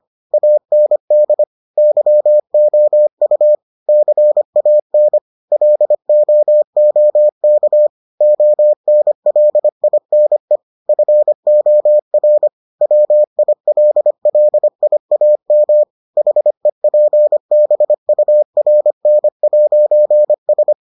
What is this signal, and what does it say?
Morse code, heard as a keyed tone, about 25 wpm, 600 Hz. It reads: AND YOU CAN LOOK ONLINE FOR WILLIAM HEPBURN'S